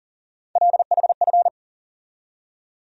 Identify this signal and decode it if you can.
Morse code, heard as a keyed tone, about 40 wpm, 700 Hz. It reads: LHF
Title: USA Call Sign Postfixes - 3 Letters 40wpm